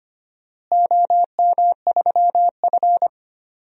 Morse code, heard as a keyed tone, about 25 wpm, 700 Hz. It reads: OM3F
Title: DXCC-100 Call Signs - 4 Characters - Repeated 2x - 25wpm